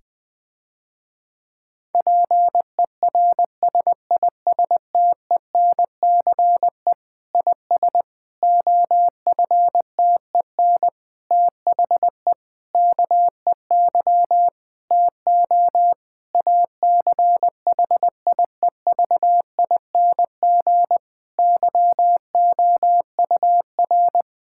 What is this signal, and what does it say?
Morse code, heard as a keyed tone, about 20 wpm, 700 Hz. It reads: PERSISTENCE IS OFTEN THE KEY TO ACHIEVING YOUR
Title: Difficult Sentences with Hints - 20wpm